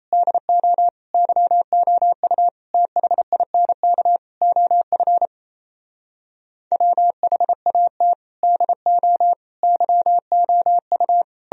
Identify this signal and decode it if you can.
Morse code, heard as a keyed tone, about 37 wpm, 700 Hz. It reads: DO YOU THINK OF WHAT DO YOU